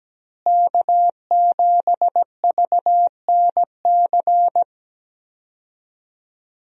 Morse code, heard as a keyed tone, about 17 wpm, 700 Hz. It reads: K7VNC
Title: USA Call Signs - 5 Characters - Repeated 3x - 17wpm